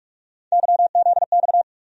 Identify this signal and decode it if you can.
Morse code, heard as a keyed tone, about 47 wpm, 700 Hz. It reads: YZX
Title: USA Call Sign Postfixes - 3 Letters 45wpm